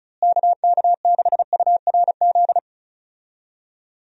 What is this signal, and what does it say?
Morse code, heard as a keyed tone, about 36 wpm, 700 Hz. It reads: KK6URZ
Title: USA Call Signs - 6 Characters - Repeated 3x - 35wpm